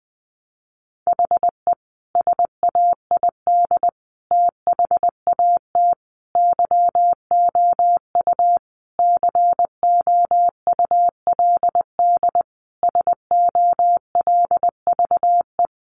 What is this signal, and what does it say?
Morse code, heard as a keyed tone, about 20 wpm, 700 Hz. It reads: HE SAID THAT YOU COULD SOLVE
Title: Sentences from Top 1000 Words 20wpm